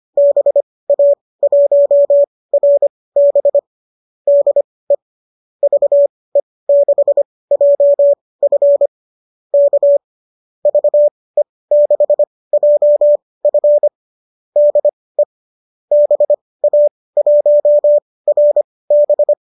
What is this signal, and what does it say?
Morse code, heard as a keyed tone, about 25 wpm, spaced as for 18 wpm, 570 Hz. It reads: BA1RB DE VE6JF K VE6JF DE BA1RB
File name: qso203 18